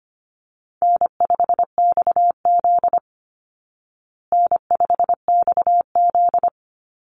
Morse code, heard as a keyed tone, about 25 wpm, 700 Hz. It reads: N5XZ N5XZ
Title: Easy Call Signs Repeated 3x - 25wpm